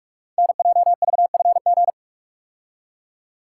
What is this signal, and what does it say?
Morse code, heard as a keyed tone, about 44 wpm, 690 Hz. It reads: NJVFG
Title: Code Groups - 5 Letters - 45wpm